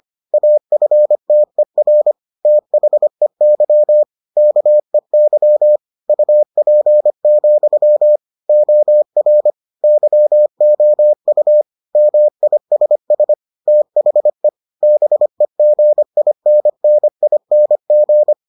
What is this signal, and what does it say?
Morse code, heard as a keyed tone, about 25 wpm, 600 Hz. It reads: AFTER THEY KEY UP, OR YOU MISS THE BEGINNING